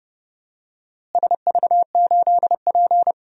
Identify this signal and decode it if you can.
Morse code, heard as a keyed tone, about 30 wpm, 700 Hz. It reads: SV8P
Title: DXCC-100 Call Signs - 4 Characters - Repeated 2x - 30wpm